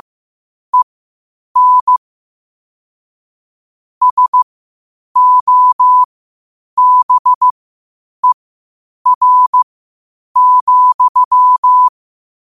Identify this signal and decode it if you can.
Morse code, heard as a keyed tone, about 15 wpm, spaced as for 8 wpm, 1000 Hz. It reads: EN SOBER,
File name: qotdfiles / 8wpm